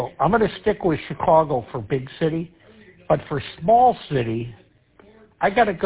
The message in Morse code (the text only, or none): none